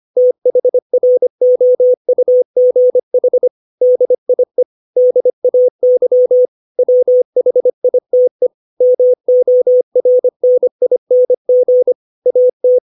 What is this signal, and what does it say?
Morse code, heard as a keyed tone, about 25 wpm, 500 Hz. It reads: THROUGH DIE DAY WHITE MORNING AT
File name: 632 words / 632 Words 25WPM